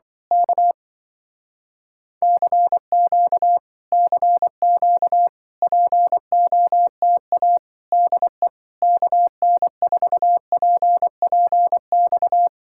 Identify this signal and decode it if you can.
Morse code, heard as a keyed tone, about 25 wpm, 700 Hz. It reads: K CQ CQ POTA DE KN4PPX
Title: Speed-Racing Common QSO Phrases - 20wpm